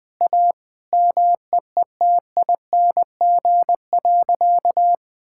Morse code, heard as a keyed tone, about 19 wpm, 700 Hz. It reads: A MEETING.
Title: Sentences with 3-Word Phrases - Spoken Beginning - 20wpm